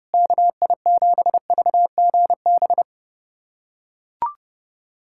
Morse code, heard as a keyed tone, about 30 wpm, 700 Hz. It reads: KI7VGB
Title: USA Callsign DE Callsign - 6 Characters - 30wpm